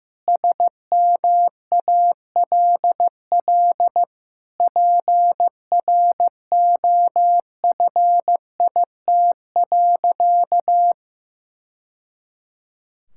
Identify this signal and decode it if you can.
Morse code, heard as a keyed tone, about 16 wpm, 700 Hz. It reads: SMALL PROFIT.
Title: Sentences from Top 1000 Words - Review Part 5 - 15wpm